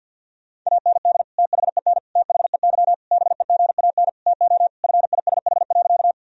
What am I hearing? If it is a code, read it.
Morse code, heard as a keyed tone, about 52 wpm, 700 Hz. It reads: AND THEN THEY BEGAN TO FISH.
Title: Sentences from Top 1000 Words - Review Part 1 - 50wpm